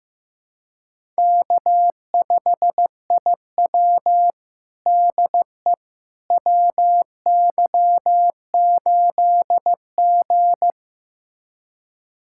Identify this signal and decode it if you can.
Morse code, heard as a keyed tone, about 15 wpm, 700 Hz. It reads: K5IW DE WY8G
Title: USA Callsign DE Callsign - 4 Characters - 15wpm